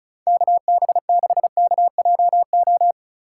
Morse code, heard as a keyed tone, about 43 wpm, 700 Hz. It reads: KB6KJO
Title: Speed Racing - USA 2x3 Call Signs 35wpm